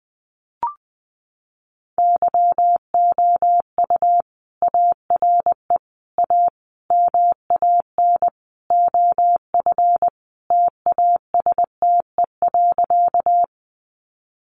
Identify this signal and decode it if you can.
Morse code, heard as a keyed tone, about 19 wpm, 700 Hz. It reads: YOU ARE A MAN OF TASTE.
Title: Sentences from Top 1100 Words 20wpm